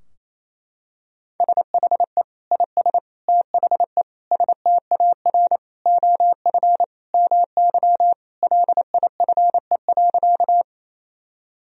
Morse code, heard as a keyed tone, about 27 wpm, 700 Hz. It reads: SHE IS THE STAR OF MY LIFE.
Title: Sentences from Top 1000 Words - Review Part 2 - 28wpm